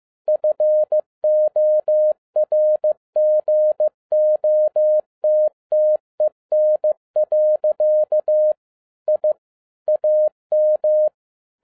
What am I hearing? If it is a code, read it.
Morse code, heard as a keyed tone, about 15 wpm, 610 Hz. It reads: FORGOTTEN. I AM
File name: 2011-10-mp3 / n02-FIELD DAY CONTEST QRV FROM SAKAE VILLAGE 15wpm